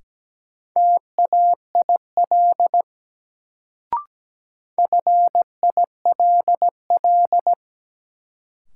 Morse code, heard as a keyed tone, about 17 wpm, 700 Hz. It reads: TAIL FILL
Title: Sentences from Top 1000 Words - Review Part 3 - New Word Drill - 17wpm